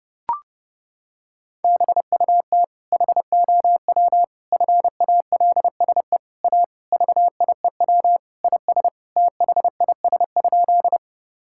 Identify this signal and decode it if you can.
Morse code, heard as a keyed tone, about 30 wpm, 700 Hz. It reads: BUT HOW FALSE A VIEW IS THIS?
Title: Sentences from Top 1600 Words 30wpm